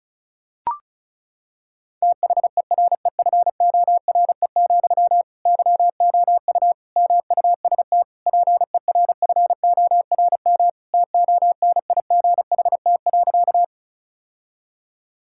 Morse code, heard as a keyed tone, about 34 wpm, 690 Hz. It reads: THEREFORE, YOU MUST PERFORM TONIGHT.
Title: Sentences from Top 1000 Words - Review Part 5 - 35wpm